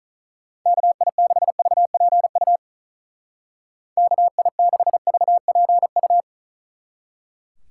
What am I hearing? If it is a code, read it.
Morse code, heard as a keyed tone, about 45 wpm, 700 Hz. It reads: KI6VPU KI6VPU